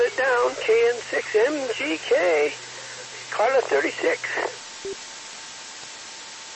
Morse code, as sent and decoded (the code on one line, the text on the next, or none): none